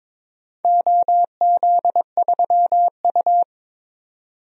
Morse code, heard as a keyed tone, about 22 wpm, 700 Hz. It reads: OZ3U
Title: DXCC-100 Call Signs - 4 Characters - 22wpm